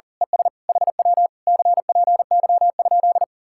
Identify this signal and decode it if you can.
Morse code, heard as a keyed tone, about 39 wpm, 700 Hz. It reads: ES HW CPY?
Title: Speed-Racing Common QSO Phrases Encore - 40wpm